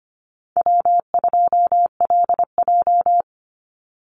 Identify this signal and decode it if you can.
Morse code, heard as a keyed tone, about 25 wpm, 700 Hz. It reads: W2LJ
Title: Easy Call Signs Repeated 3x - 25wpm